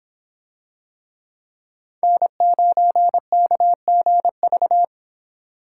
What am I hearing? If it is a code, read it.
Morse code, heard as a keyed tone, about 26 wpm, 700 Hz. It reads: N9KGV